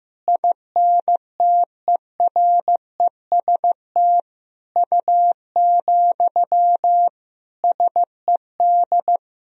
Morse code, heard as a keyed tone, about 16 wpm, 700 Hz. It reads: INTEREST U, SED
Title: Sentences from Top 400 Words and Abbreviations 15wpm